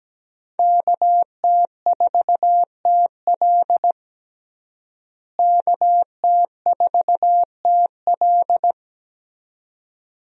Morse code, heard as a keyed tone, about 17 wpm, 700 Hz. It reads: KT4TL KT4TL
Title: USA Call Signs - 5 Characters - Repeated 2x - 17wpm